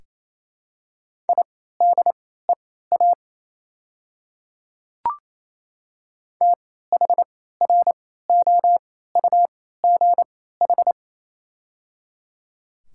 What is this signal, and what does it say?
Morse code, heard as a keyed tone, about 27 wpm, spaced as for 16 wpm, 690 Hz. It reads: IDEA THROUGH
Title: 3x Character Spacing - Top 300 Words - 28wpm